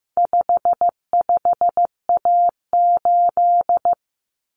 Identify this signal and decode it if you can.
Morse code, heard as a keyed tone, about 15 wpm, 700 Hz. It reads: 55A8
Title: Common Cut Numbers - 4-Digit 15wpm